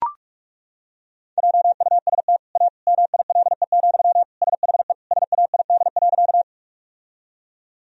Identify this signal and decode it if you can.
Morse code, heard as a keyed tone, about 41 wpm, 700 Hz. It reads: JUST A MILE, SHE SAID.
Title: Sentences from Top 1000 Words - Review Part 1 - 45wpm